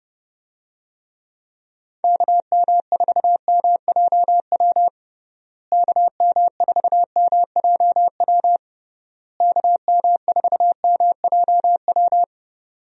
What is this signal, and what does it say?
Morse code, heard as a keyed tone, about 32 wpm, 700 Hz. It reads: KM4MJW KM4MJW KM4MJW
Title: USA Call Signs - 6 Characters - Repeated 3x - 30wpm